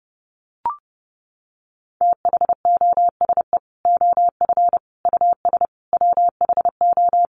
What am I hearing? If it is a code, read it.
Morse code, heard as a keyed tone, about 30 wpm, 700 Hz. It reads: THOSE OF US WHO
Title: Sets of 5 Words Encore2 30wpm